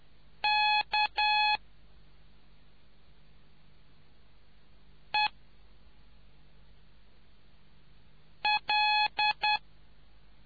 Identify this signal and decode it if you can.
Morse code, heard as a keyed tone, about 10 wpm, 840 Hz. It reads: K E L